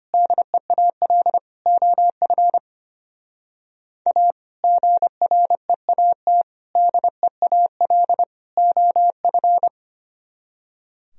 Morse code, heard as a keyed tone, about 32 wpm, 700 Hz. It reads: DEAL OF A GREAT DEAL OF